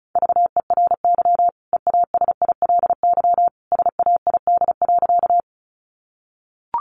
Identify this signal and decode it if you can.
Morse code, heard as a keyed tone, about 34 wpm, 700 Hz. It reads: VERY EASILY SAID.
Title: Sentences from Top 900 Words 35wpm